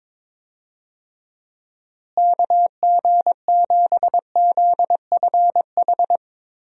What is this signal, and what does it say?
Morse code, heard as a keyed tone, about 21 wpm, 700 Hz. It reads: KG7ZFH
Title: USA Call Signs - 6 Characters - Repeated 2x - 22wpm